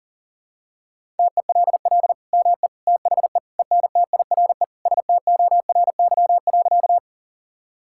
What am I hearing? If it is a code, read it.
Morse code, heard as a keyed tone, about 40 wpm, 700 Hz. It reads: TELL ME THE ENTIRE STORY.
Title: Sentences from Top 1000 Words - Review Part 5 - 40wpm